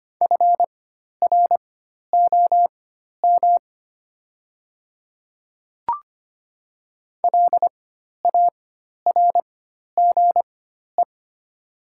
Morse code, heard as a keyed tone, about 25 wpm, spaced as for 12 wpm, 700 Hz. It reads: FROM LARGE
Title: 4x Character Spacing - Top 500 Words - 25wpm